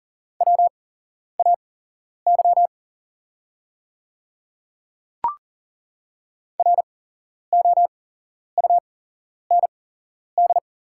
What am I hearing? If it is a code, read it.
Morse code, heard as a keyed tone, about 40 wpm, spaced as for 11 wpm, 700 Hz. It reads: WAY ROUND
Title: 8x Character Spacing - Top 300 Words - 40wpm